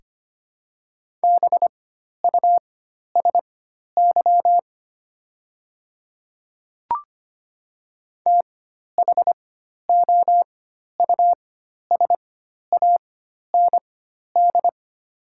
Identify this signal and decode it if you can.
Morse code, heard as a keyed tone, about 25 wpm, spaced as for 12 wpm, 690 Hz. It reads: BUSY THOUSAND
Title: 4x Character Spacing - Top 500 Words - 25wpm